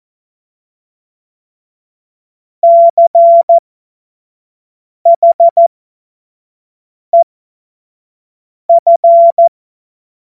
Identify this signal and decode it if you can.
Morse code, heard as a keyed tone, about 14 wpm, spaced as for 5 wpm, 680 Hz. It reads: CHEF
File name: Morsekurs / Witze / Witz 29 April 2025 25 F70